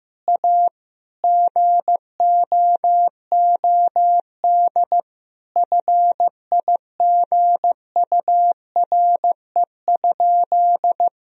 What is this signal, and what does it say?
Morse code, heard as a keyed tone, about 15 wpm, 700 Hz. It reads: A GOOD FIGURE?
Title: Sentences from Top 1000 Words - Review Part 2 - 15wpm